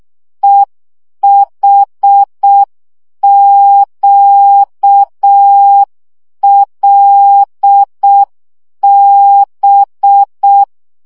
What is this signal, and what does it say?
Morse code, heard as a keyed tone, about 6 wpm, 800 Hz. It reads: EHQLB
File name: morze vizsga 2011-1 5